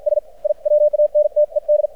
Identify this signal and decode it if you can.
Morse code, heard as a keyed tone, about 53 wpm, spaced as for 47 wpm, 600 Hz. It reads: S I WANTED
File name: archived / CRSnet